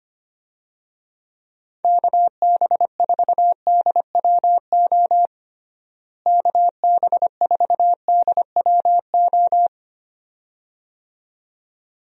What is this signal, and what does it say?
Morse code, heard as a keyed tone, about 25 wpm, 700 Hz. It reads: KB4DWO KB4DWO
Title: USA Call Signs - 6 Characters - Repeated 2x - 25wpm